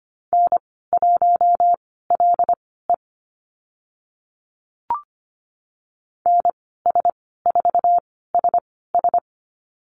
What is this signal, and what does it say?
Morse code, heard as a keyed tone, about 25 wpm, spaced as for 16 wpm, 700 Hz. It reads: N1LE NS4SS
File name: Lesson 15g - Number 5 - Call Signs 25wpm